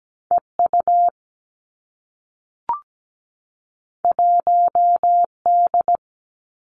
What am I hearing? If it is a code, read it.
Morse code, heard as a keyed tone, about 17 wpm, 700 Hz. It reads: EU 1D
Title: Two Challenging Letter-Number Combinations 17wpm